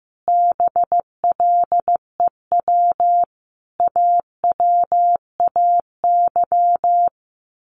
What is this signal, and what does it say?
Morse code, heard as a keyed tone, about 15 wpm, 700 Hz. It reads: BLEW AWAY
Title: A Sound Of Thunder 15wpm